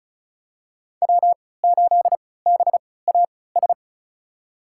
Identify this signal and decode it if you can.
Morse code, heard as a keyed tone, about 34 wpm, spaced as for 20 wpm, 700 Hz. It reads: W8BAS
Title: USA Call Signs - 5 Characters - 3x Character Spacing - 35wpm